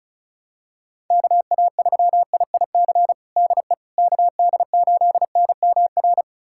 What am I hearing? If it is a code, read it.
Morse code, heard as a keyed tone, about 34 wpm, 700 Hz. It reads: KA3IIC DE KD8NMR
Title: USA Callsign DE Callsign - 6 Characters - 35wpm